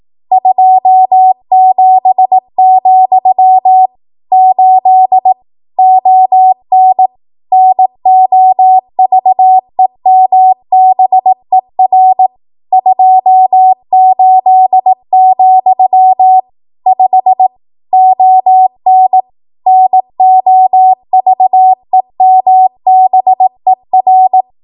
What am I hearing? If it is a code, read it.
Morse code, heard as a keyed tone, about 18 wpm, 750 Hz. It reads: 27, 8 ON NOVEMBER 28, 5 ON NOVEMBER
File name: Archive / 18 WPM / 18WPM